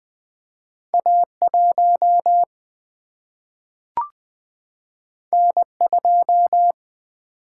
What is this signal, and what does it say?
Morse code, heard as a keyed tone, about 20 wpm, 700 Hz. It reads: A 1 N 2
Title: Single-Letter Single-Number 20wpm